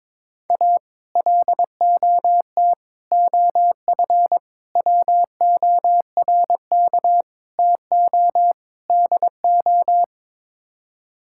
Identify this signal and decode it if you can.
Morse code, heard as a keyed tone, about 22 wpm, 700 Hz. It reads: A LOT OF WORK TO DO
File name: practice / 22wpm / Sets of 6 Words - 22wpm